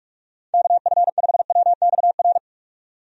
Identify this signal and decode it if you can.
Morse code, heard as a keyed tone, about 46 wpm, 700 Hz. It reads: KF5WXR
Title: USA Call Signs - 6 Characters - 45wpm